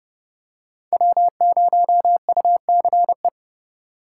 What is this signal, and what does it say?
Morse code, heard as a keyed tone, about 30 wpm, 700 Hz. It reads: W0UCE
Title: K1USN SST QSO Practice - 30wpm